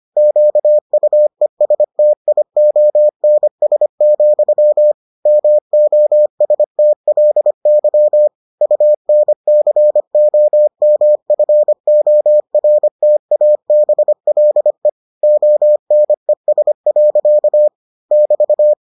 Morse code, heard as a keyed tone, about 25 wpm, 600 Hz. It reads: QUESTIONS, MOSTLY UNCOMFORTABLE ONES. <BT>